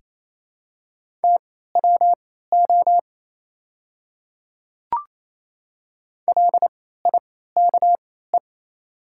Morse code, heard as a keyed tone, about 28 wpm, spaced as for 16 wpm, 700 Hz. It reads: TWO LIKE
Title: 3x Character Spacing - Top 100 Words - 28wpm